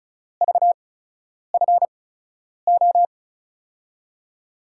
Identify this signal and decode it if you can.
Morse code, heard as a keyed tone, about 34 wpm, spaced as for 10 wpm, 700 Hz. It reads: VFO
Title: USA Call Signs - 6 Characters - 8x Character Spacing - 35wpm